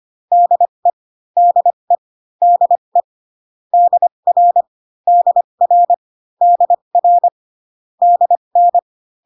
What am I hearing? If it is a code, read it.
Morse code, heard as a keyed tone, about 25 wpm, spaced as for 22 wpm, 700 Hz. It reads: DE DE DE DR DR DR DN
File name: CW common abbreviations alphabetical repeated 3 times 25 18wpm